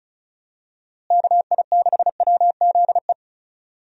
Morse code, heard as a keyed tone, about 36 wpm, 700 Hz. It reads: KI6WZE
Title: USA Call Signs - 6 Characters - Repeated 2x - 35wpm